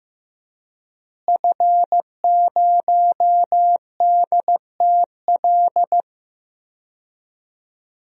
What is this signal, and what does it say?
Morse code, heard as a keyed tone, about 15 wpm, 700 Hz. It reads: F0DTL
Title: DXCC-100 Call Signs - 5 Characters - Repeated 3x - 15wpm